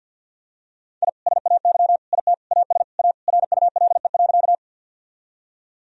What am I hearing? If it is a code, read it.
Morse code, heard as a keyed tone, about 49 wpm, 700 Hz. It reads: I SAY IT AS A RULE.
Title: Sentences from Top 1000 Words - Review Part 2 - 50wpm